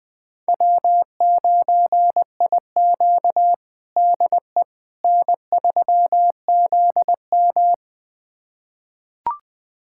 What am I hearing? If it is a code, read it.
Morse code, heard as a keyed tone, about 20 wpm, 700 Hz. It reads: W9IQ DE N3ZM
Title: USA Callsign DE Callsign - 4 Characters - 20wpm